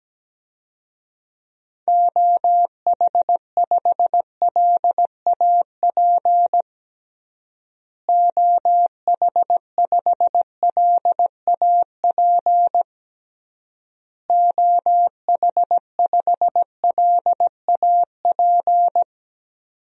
Morse code, heard as a keyed tone, about 17 wpm, 700 Hz. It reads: OH5LAP OH5LAP OH5LAP
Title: DXCC-100 Call Signs - 6 Characters - Repeated 3x - 17wpm